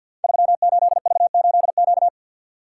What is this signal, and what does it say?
Morse code, heard as a keyed tone, about 53 wpm, 700 Hz. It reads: 38V8Q
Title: Code Groups - 5 Letter:Numbers - 50wpm